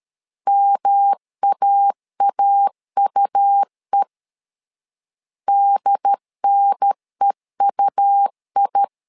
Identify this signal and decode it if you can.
Morse code, heard as a keyed tone, about 13 wpm, 800 Hz. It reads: MAAUE DNEUI
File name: LEZIONI / 3 MWRO / 3 M 12wpm 1541